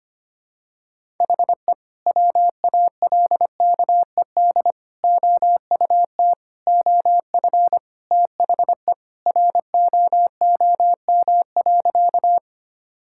Morse code, heard as a keyed tone, about 25 wpm, 700 Hz. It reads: HE WALKED OUT OF THE ROOM.
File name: practice / 25wpm / Sentences with 2-Word Phrases - 25wpm